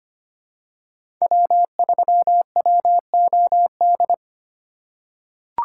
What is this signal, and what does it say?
Morse code, heard as a keyed tone, about 25 wpm, 700 Hz. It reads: W3WOD
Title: USA Call Signs - 5 Characters - 3x Character Spacing - 25wpm